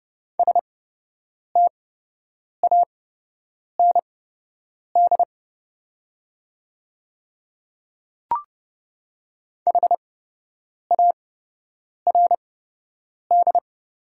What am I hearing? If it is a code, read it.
Morse code, heard as a keyed tone, about 29 wpm, spaced as for 8 wpm, 700 Hz. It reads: STAND HARD
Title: 8x Character Spacing - Top 500 Words - 30wpm